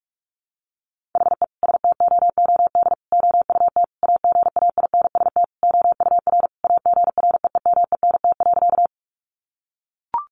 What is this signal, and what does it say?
Morse code, heard as a keyed tone, about 44 wpm, 700 Hz. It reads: HE STOOD OUT AGAINST OUR AGREEMENT.
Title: Sentences from Top 1500 Words 45wpm